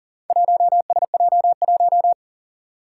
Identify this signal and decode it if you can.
Morse code, heard as a keyed tone, about 39 wpm, 700 Hz. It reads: 1SJ1
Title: Four Challenging Letter-Number Combinations 40wpm